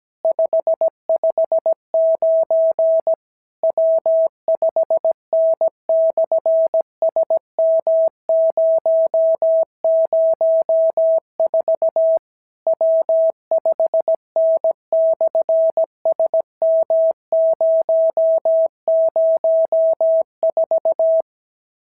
Morse code, heard as a keyed tone, about 17 wpm, 650 Hz. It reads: 559 W5N/SM004 W5N/SM004
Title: SOTA QSO Practice 17wpm